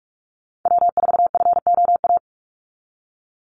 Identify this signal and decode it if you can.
Morse code, heard as a keyed tone, about 44 wpm, 700 Hz. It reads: W4FOA
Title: Call Signs Repeated 3x - 45wpm